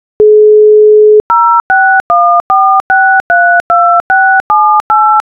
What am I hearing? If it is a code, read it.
DTMF, heard as telephone touch tones, tones of about 330 ms. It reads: 0614632678